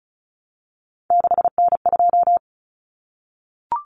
Morse code, heard as a keyed tone, about 34 wpm, 700 Hz. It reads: 6N2
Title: Common Cut Numbers - 3-Digit 35wpm